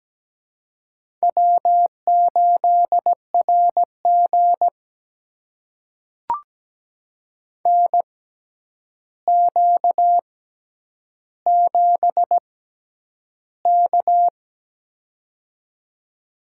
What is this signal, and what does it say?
Morse code, heard as a keyed tone, about 17 wpm, spaced as for 9 wpm, 700 Hz. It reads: W8RG NQ7K
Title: USA Call Signs - 6x Character Spacing - 17wpm